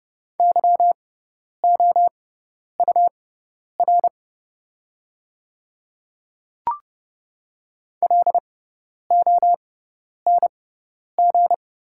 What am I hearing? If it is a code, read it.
Morse code, heard as a keyed tone, about 29 wpm, spaced as for 10 wpm, 700 Hz. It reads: YOUR LONG
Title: 6x Character Spacing - Top 100 Words - 30wpm